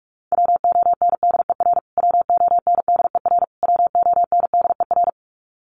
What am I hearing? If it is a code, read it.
Morse code, heard as a keyed tone, about 44 wpm, 700 Hz. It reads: WONDER WONDER WONDER
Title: Top 500 Words Repeated 45wpm